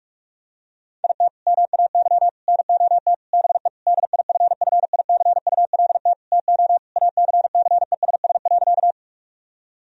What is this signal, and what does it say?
Morse code, heard as a keyed tone, about 46 wpm, 700 Hz. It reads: IT MAY NOT BE DIFFICULT TO ACCESS.
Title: Sentences from Top 1000 Words - Review Part 4 - 45wpm